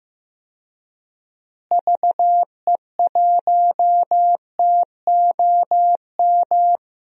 Morse code, heard as a keyed tone, about 16 wpm, 700 Hz. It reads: VE1TOM
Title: DXCC-100 Call Signs - 6 Characters - Repeated 3x - 15wpm